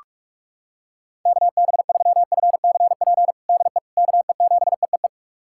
Morse code, heard as a keyed tone, about 45 wpm, 700 Hz. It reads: KB3FCP DE KE7EEE
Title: USA Callsign DE Callsign 45wpm